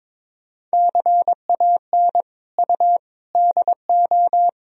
Morse code, heard as a keyed tone, about 22 wpm, 700 Hz. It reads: CAN U DO
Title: Sentences from Top 100 Words and Abbreviations 22wpm